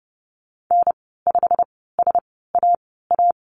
Morse code, heard as a keyed tone, about 30 wpm, 700 Hz. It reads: N 5 S A A
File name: Lesson 19g - Letter w - Call Signs 30wpm